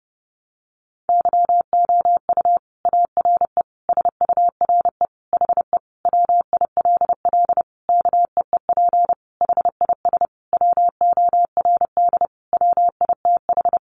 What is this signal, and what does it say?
Morse code, heard as a keyed tone, about 30 wpm, 700 Hz. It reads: YOU ARE SURE HE WILL KEEP HIS WORD WITH
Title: Sentences from Top 300 Words 30wpm